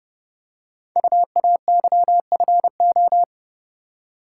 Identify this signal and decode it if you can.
Morse code, heard as a keyed tone, about 30 wpm, 700 Hz. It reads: UAYFO
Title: Code Groups - 5 Letters - 30wpm